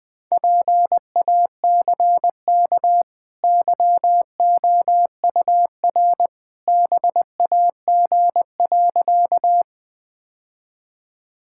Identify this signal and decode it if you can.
Morse code, heard as a keyed tone, about 20 wpm, 700 Hz. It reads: PACK YOUR BAG.
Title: Easy Sentences with Hints - 20wpm